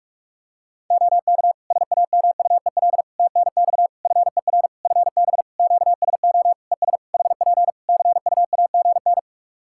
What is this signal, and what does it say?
Morse code, heard as a keyed tone, about 46 wpm, 700 Hz. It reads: OK SAMUEL TNX FER FB QSO ES HP CUAGN
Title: K1USN SST QSO Practice 45wpm